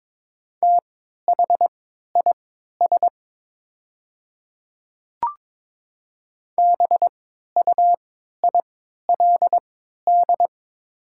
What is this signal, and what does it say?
Morse code, heard as a keyed tone, about 22 wpm, spaced as for 13 wpm, 700 Hz. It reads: THIS BUILD
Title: 3x Character Spacing - Top 200 Words - 22wpm